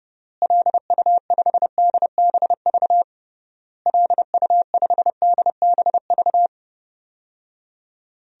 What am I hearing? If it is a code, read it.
Morse code, heard as a keyed tone, about 30 wpm, 700 Hz. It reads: LU5DBV LU5DBV
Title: DXCC-100 Call Signs - 6 Characters - Repeated 3x - 30wpm